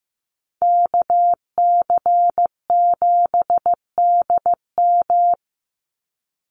Morse code, heard as a keyed tone, about 15 wpm, 700 Hz. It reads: KC7DM
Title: NA SOTA Activator Callsigns 15wpm